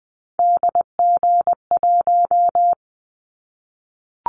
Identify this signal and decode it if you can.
Morse code, heard as a keyed tone, about 20 wpm, 700 Hz. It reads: DG1
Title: Callsign Prefixes for Top 25 DXCC - 20wpm